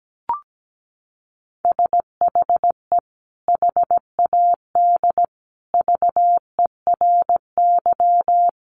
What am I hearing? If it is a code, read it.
Morse code, heard as a keyed tone, about 17 wpm, 700 Hz. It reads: SHE HAD VERY
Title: Sentences from Top 400 Words 17wpm